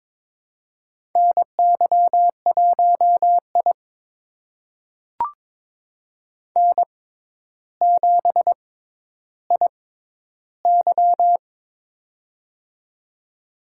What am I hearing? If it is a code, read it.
Morse code, heard as a keyed tone, about 23 wpm, spaced as for 11 wpm, 700 Hz. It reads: NY1I N7IY
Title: USA Call Signs - 4 Characters - 6x Character Spacing - 22wpm